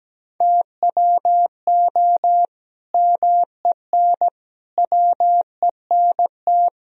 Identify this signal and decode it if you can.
Morse code, heard as a keyed tone, about 17 wpm, 700 Hz. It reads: TWO MEN WENT
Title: Sentences from Top 1000 Words - Review Part 3 - 17wpm